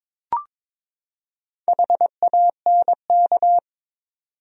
Morse code, heard as a keyed tone, about 22 wpm, 700 Hz. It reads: HANK
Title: USA Common Names - Max 4 Letters - 22wpm